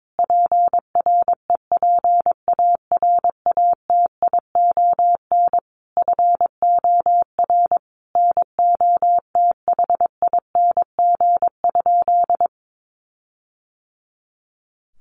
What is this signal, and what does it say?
Morse code, heard as a keyed tone, about 22 wpm, 700 Hz. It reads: PREPARATION FOR NOTHING?
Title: Sentences from Top 1400 Words 22wpm